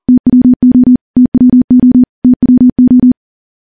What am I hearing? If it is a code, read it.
Morse code, heard as a keyed tone, about 40 wpm, 250 Hz. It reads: TWO TWO TWO
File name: morse / morse-training-40WPM